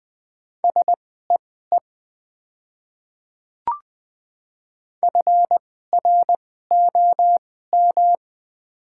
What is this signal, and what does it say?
Morse code, heard as a keyed tone, about 20 wpm, spaced as for 14 wpm, 700 Hz. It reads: SEE FROM
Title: 2x Character Spacing - Top 100 Words - 20wpm